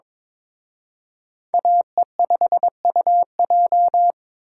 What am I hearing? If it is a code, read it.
Morse code, heard as a keyed tone, about 26 wpm, 700 Hz. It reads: AE5UJ
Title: Speed Racing - USA 2x2 Call Signs 22wpm